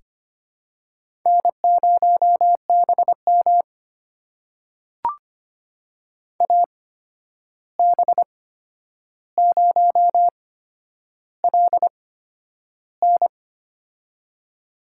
Morse code, heard as a keyed tone, about 25 wpm, spaced as for 10 wpm, 700 Hz. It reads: N0BM AB0LN
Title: USA Call Signs - 8x Character Spacing - 25wpm